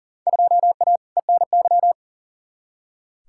Morse code, heard as a keyed tone, about 40 wpm, 700 Hz. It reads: 2A ENY
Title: Field Day Exchanges - 40wpm